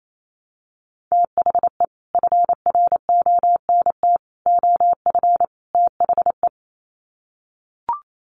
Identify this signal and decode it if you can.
Morse code, heard as a keyed tone, about 28 wpm, 700 Hz. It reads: THE FRONT OF THE
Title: Sets of 4 Words 28wpm